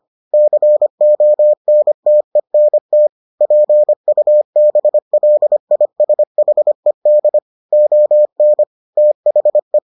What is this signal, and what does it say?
Morse code, heard as a keyed tone, about 25 wpm, 600 Hz. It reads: CONTENT PUBLISHED ON THE